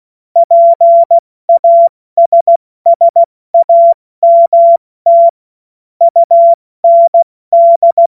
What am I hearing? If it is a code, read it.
Morse code, heard as a keyed tone, about 16 wpm, 680 Hz. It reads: PASSAMT UND
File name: Morsekurs / Witze / Witz 19. April 2022 70 F80